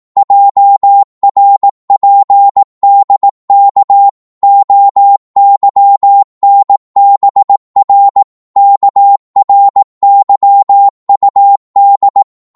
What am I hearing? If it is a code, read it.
Morse code, heard as a keyed tone, about 18 wpm, 800 Hz. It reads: JRPDK OYNBR KRYUD
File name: Morse / ltr-18wpm